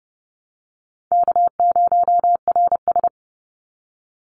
Morse code, heard as a keyed tone, about 30 wpm, 700 Hz. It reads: K0RS
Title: NA SOTA Chaser Callsigns 30wpm